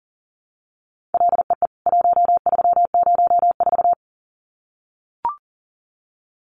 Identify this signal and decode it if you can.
Morse code, heard as a keyed tone, about 40 wpm, 700 Hz. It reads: LEE 1304
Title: CWT Exchange 40wpm